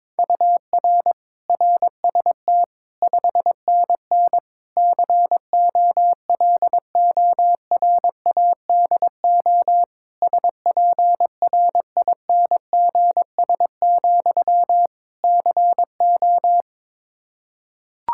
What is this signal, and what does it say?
Morse code, heard as a keyed tone, about 22 wpm, 700 Hz. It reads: UR RST 5NN COLORADO SPRINGS, CO
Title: Practice QSO - Exchange 1 - Beginning - 22wpm